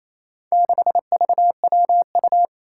Morse code, heard as a keyed tone, about 28 wpm, 700 Hz. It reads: 6VWU
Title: Four Challenging Letter-Number Combinations 28wpm